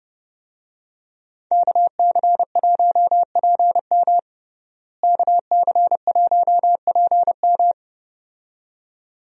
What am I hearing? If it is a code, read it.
Morse code, heard as a keyed tone, about 30 wpm, 700 Hz. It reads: KC1PM KC1PM
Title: USA Call Signs - 5 Characters - Repeated 2x - 30wpm